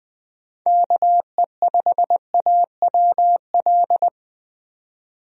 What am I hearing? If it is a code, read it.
Morse code, heard as a keyed tone, about 20 wpm, 700 Hz. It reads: KE5AWL
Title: USA Call Signs - 6 Characters - 20wpm